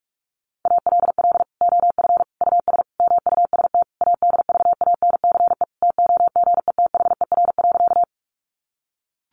Morse code, heard as a keyed tone, about 44 wpm, 700 Hz. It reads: ALL OF US MUST ADVANCE TOGETHER.
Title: Sentences from Top 1400 Words 45wpm